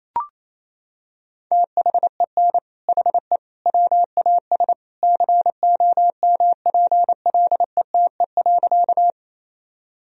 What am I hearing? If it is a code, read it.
Morse code, heard as a keyed tone, about 28 wpm, 700 Hz. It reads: THEN HE WAS COMPLETE.
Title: Sentences from Top 1000 Words - Review Part 2 - 28wpm